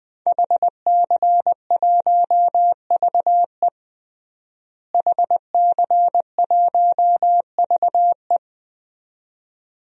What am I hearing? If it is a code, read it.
Morse code, heard as a keyed tone, about 20 wpm, 700 Hz. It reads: HC1VE HC1VE
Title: DXCC-100 Call Signs - 5 Characters - Repeated 2x - 20wpm